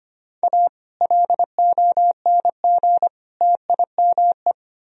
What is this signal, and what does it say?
Morse code, heard as a keyed tone, about 24 wpm, 700 Hz. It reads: A LONG TIME
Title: Speed-Racing - Sets of 3 Words 25wpm